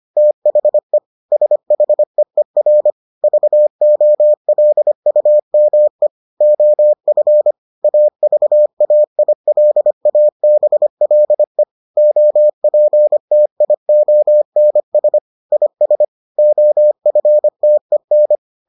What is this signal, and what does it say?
Morse code, heard as a keyed tone, about 25 wpm, 600 Hz. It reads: THE SHEER VOLUME OF AVAILABLE OPTIONS IS OFTEN